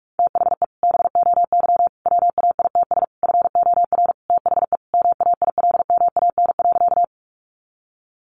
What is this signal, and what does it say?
Morse code, heard as a keyed tone, about 44 wpm, 700 Hz. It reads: THE BOY WAITS FOR THE MAILMAN.
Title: Easy Phrases 45wpm